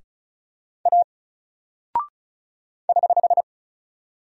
Morse code, heard as a keyed tone, about 36 wpm, 700 Hz. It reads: A <HH>
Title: Single Character (Letter:Number:Punctuation:Prosign) 35wpm